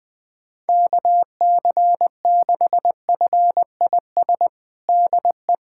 Morse code, heard as a keyed tone, about 21 wpm, 700 Hz. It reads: KC6FIS DE
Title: USA Callsign DE Callsign - 6 Characters - 20wpm